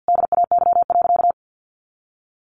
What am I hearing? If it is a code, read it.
Morse code, heard as a keyed tone, about 50 wpm, 700 Hz. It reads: DAY.
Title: Sentences from Top 1100 Words 50wpm